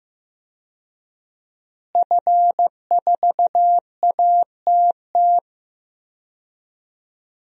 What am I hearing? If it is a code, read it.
Morse code, heard as a keyed tone, about 16 wpm, 700 Hz. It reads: F4ATT
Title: DXCC-100 Call Signs - 5 Characters - Repeated 3x - 15wpm